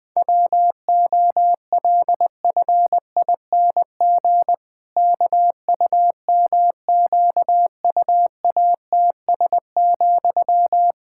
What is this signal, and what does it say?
Morse code, heard as a keyed tone, about 20 wpm, 700 Hz. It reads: WOLFING KUMQUATS,
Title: Pangram Challenge v2 - 20wpm